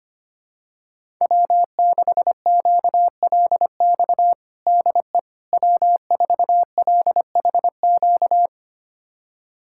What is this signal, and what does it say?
Morse code, heard as a keyed tone, about 25 wpm, 700 Hz. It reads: W6QLX DE W4LHQ
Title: USA Callsign DE Callsign - 5 Characters - 25wpm